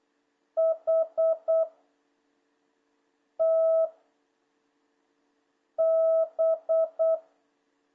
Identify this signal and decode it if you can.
Morse code, heard as a keyed tone, about 8 wpm, 660 Hz. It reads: H T B
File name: jh8cbh / 2026cw